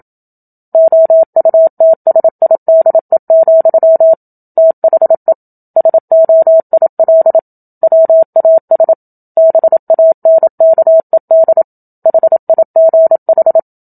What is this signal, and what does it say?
Morse code, heard as a keyed tone, about 27 wpm, 650 Hz. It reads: OUTSIDE, THE SOIL WAS BANKED HIGH
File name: Book Two / 01-Chapter 1 - 27 WPM